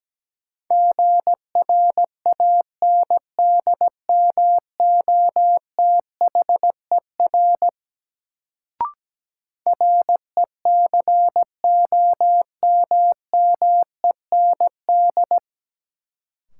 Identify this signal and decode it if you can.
Morse code, heard as a keyed tone, about 17 wpm, 700 Hz. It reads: GRANDMOTHER RECOMMEND
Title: Sentences from Top 1200 Words - New Word Drill - 17wpm